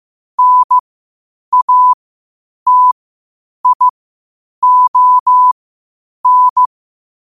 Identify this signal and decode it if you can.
Morse code, heard as a keyed tone, about 15 wpm, spaced as for 8 wpm, 1000 Hz. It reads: NATION
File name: qotdfiles / 8wpm